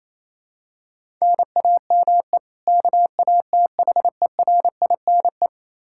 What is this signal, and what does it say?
Morse code, heard as a keyed tone, about 28 wpm, 700 Hz. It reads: NAME KATHERINE
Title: Practice QSO - Exchange 1 - Beginning - 28wpm